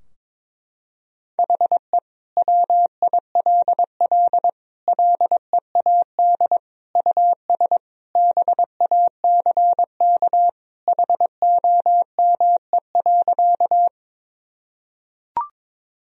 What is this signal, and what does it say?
Morse code, heard as a keyed tone, about 22 wpm, 700 Hz. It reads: HE WILL LEAD US BACK HOME.
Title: Sentences from Top 1000 Words - Review Part 2 - 22wpm